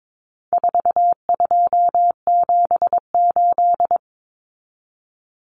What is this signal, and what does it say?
Morse code, heard as a keyed tone, about 22 wpm, 700 Hz. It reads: 4278
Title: Numbers - 4-Digit 22wpm